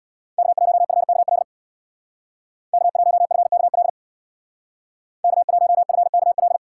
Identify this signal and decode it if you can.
Morse code, heard as a keyed tone, about 64 wpm, 700 Hz. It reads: K1FKL K1FKL K1FKL